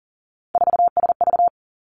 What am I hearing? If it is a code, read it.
Morse code, heard as a keyed tone, about 42 wpm, 700 Hz. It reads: 4SV
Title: Three Challenging Letter-Number Combinations 40wpm